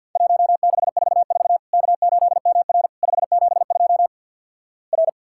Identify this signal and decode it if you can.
Morse code, heard as a keyed tone, about 48 wpm, 700 Hz. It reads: 1644 K8MR 572 R
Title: Worked All Europe DX Contest - QTC Exchange with Repeats - 50wpm